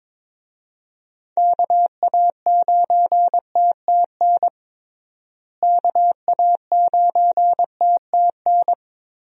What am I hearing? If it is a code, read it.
Morse code, heard as a keyed tone, about 22 wpm, 700 Hz. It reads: KA9TTN KA9TTN
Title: USA Call Signs - 6 Characters - Repeated 2x - 22wpm